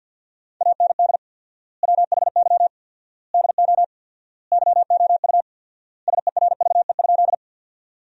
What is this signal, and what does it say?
Morse code, heard as a keyed tone, about 47 wpm, 700 Hz. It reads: AND WHY DO YOU SERVE?
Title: 4x Word-Spacing - Sentences from Top 400 Words - 50wpm